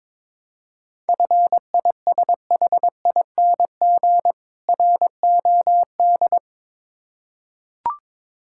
Morse code, heard as a keyed tone, about 22 wpm, 700 Hz. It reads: FISHING ROD
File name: Difficult Words with Hints - 22wpm